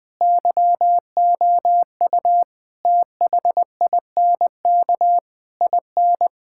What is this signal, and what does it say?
Morse code, heard as a keyed tone, about 20 wpm, 700 Hz. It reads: YOU THINK IN
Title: Sets of 8 Words - 20wpm